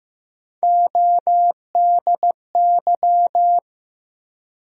Morse code, heard as a keyed tone, about 15 wpm, 700 Hz. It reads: ODY
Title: USA Call Sign Postfixes - 3 Letters 15wpm